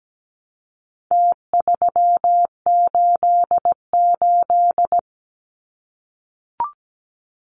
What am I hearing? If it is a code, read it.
Morse code, heard as a keyed tone, about 17 wpm, 700 Hz. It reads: T388
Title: Numbers - Contest Style - 4-Digit 17wpm